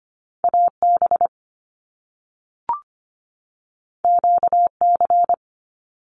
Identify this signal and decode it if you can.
Morse code, heard as a keyed tone, about 25 wpm, 700 Hz. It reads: AB QC
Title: Canadian Provinces and Territories 25wpm